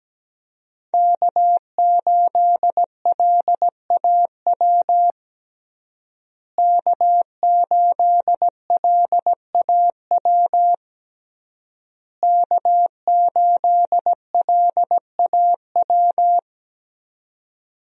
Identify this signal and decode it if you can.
Morse code, heard as a keyed tone, about 17 wpm, 700 Hz. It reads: K8LAW K8LAW K8LAW
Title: USA Call Signs - 5 Characters - Repeated 3x - 17wpm